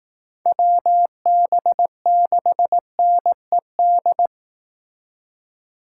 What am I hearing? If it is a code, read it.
Morse code, heard as a keyed tone, about 18 wpm, 700 Hz. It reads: WB6NED
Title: Speed Racing - USA 2x3 Call Signs 15wpm